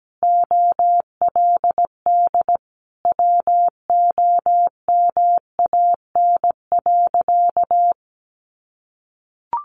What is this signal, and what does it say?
Morse code, heard as a keyed tone, about 17 wpm, 700 Hz. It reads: OLD WOMAN.
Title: Sentences from Top 1100 Words 17wpm